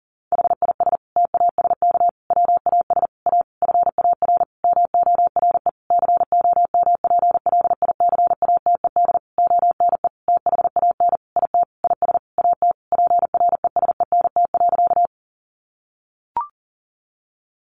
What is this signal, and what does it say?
Morse code, heard as a keyed tone, about 40 wpm, 700 Hz. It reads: HIS TASK WAS A FAR MORE COMPLICATED ONE THAN IT IS AT PRESENT.
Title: Sentences from Top 1200 Words 40wpm